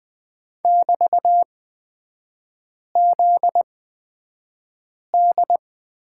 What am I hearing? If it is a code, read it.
Morse code, heard as a keyed tone, about 20 wpm, spaced as for 5 wpm, 700 Hz. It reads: <BT>ZD
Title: 243 Lesson 39e - Letter z - Three Characters 20wpm